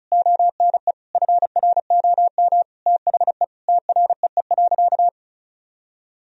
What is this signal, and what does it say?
Morse code, heard as a keyed tone, about 36 wpm, 700 Hz. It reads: ONE FROM THE TREE.
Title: Sentences from Top 1000 Words - Review Part 1 - 35wpm